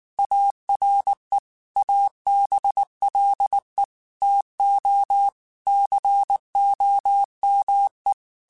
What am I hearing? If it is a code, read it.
Morse code, heard as a keyed tone, about 19 wpm, 800 Hz. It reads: ARE ABLE TO COME
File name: QSOs / QSOs-mp3s / WA7QOX 1@18